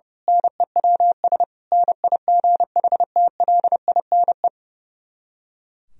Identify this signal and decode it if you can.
Morse code, heard as a keyed tone, about 30 wpm, 700 Hz. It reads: NEWS NIGHTLINE
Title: Speed-Racing - Sets of 5 Words 30wpm